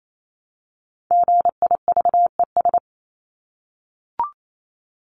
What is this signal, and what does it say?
Morse code, heard as a keyed tone, about 28 wpm, 700 Hz. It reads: GIVES
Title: ES:S Suffix - Top 100 Words with Base Words 28wpm